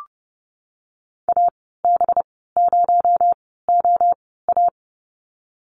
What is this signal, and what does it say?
Morse code, heard as a keyed tone, about 30 wpm, spaced as for 17 wpm, 700 Hz. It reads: AB0OA
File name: Lesson 37g - Letter x - Call Signs 30wpm